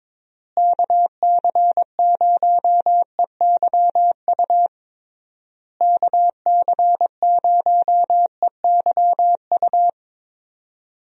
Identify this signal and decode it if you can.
Morse code, heard as a keyed tone, about 22 wpm, 700 Hz. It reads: KC0EYU KC0EYU
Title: USA Call Signs - 6 Characters - Repeated 3x - 22wpm